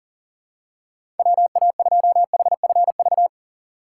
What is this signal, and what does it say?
Morse code, heard as a keyed tone, about 40 wpm, 700 Hz. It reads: WA2HFV
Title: USA Call Signs - 6 Characters - 4x Character Spacing - 40wpm